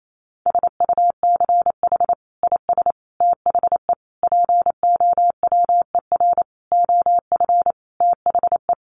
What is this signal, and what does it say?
Morse code, heard as a keyed tone, about 28 wpm, 700 Hz. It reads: SUCH IS THE POWER OF THE
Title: Sentences from Top 1400 Words 28wpm